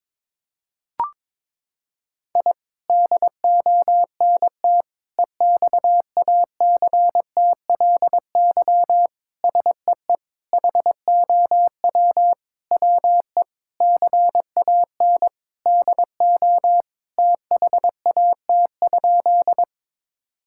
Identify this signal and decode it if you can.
Morse code, heard as a keyed tone, about 22 wpm, 700 Hz. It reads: I DONT EXACTLY SEE HOW WE CAN DO THAT?
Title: Sentences from Top 1000 Words - Review Part 6 - 22wpm